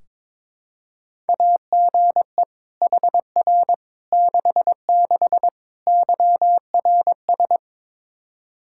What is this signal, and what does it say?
Morse code, heard as a keyed tone, about 21 wpm, 700 Hz. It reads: AGE HR 66 YRS
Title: Practice QSO - Exchange 3 - More Info - 22wpm